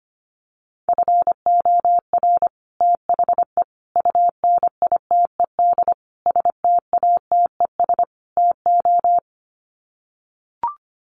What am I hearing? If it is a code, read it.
Morse code, heard as a keyed tone, about 25 wpm, 700 Hz. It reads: FOR THE UNITED STATES TO
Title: Sets of 5 Words Encore 25wpm